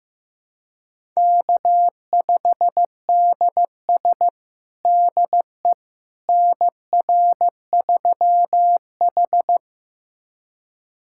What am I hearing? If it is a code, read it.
Morse code, heard as a keyed tone, about 15 wpm, 700 Hz. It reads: K5DS DE NR3H
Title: USA Callsign DE Callsign - 4 Characters - 15wpm